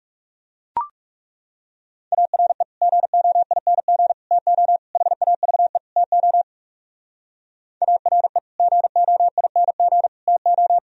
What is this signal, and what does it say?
Morse code, heard as a keyed tone, about 37 wpm, 700 Hz. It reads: ARE GOING TO HAVE TO ARE GOING TO
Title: Speed-Racing - Sets of 5 Words 30wpm